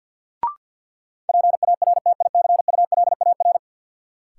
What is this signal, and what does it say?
Morse code, heard as a keyed tone, about 53 wpm, 700 Hz. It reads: PARTICULAR
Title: Sentences from Top 600 Words - New Word Drill - 50wpm